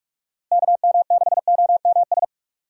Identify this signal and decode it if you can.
Morse code, heard as a keyed tone, about 47 wpm, 700 Hz. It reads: KM6OMS
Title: USA Call Signs - 6 Characters - Repeated 2x - 45wpm